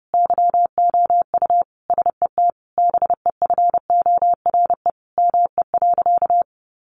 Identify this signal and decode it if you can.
Morse code, heard as a keyed tone, about 30 wpm, 700 Hz. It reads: YOU SET BEFORE ME.
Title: Sentences from Top 1300 Words 30wpm